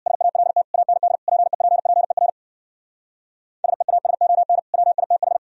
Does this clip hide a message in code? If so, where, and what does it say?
Yes: Morse code from 0.1 to 5.4 s, heard as a keyed tone, about 64 wpm, 690 Hz. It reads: SALT AND PEPPER SEASON WITH